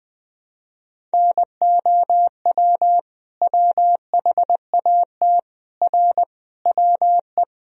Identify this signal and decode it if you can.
Morse code, heard as a keyed tone, about 20 wpm, 700 Hz. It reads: NOW WHAT R WE
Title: Sentences from Top 100 Words and Abbreviations 20wpm